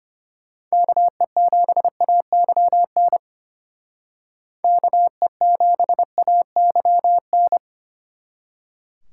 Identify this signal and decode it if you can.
Morse code, heard as a keyed tone, about 32 wpm, 700 Hz. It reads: KE7AYN KE7AYN